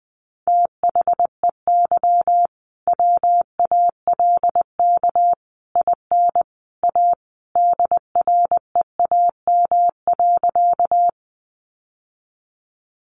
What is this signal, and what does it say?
Morse code, heard as a keyed tone, about 20 wpm, 700 Hz. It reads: THEY WALK IN A DREAM.
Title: Sentences from Top 1500 Words 20wpm